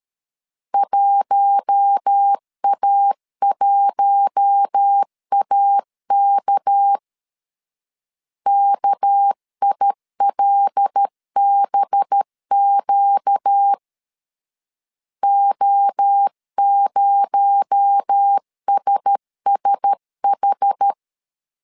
Morse code, heard as a keyed tone, about 13 wpm, 800 Hz. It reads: 1A1AK KILBQ O0SSH
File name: CW / DET / 1 6 1500C S C 12wpm 3533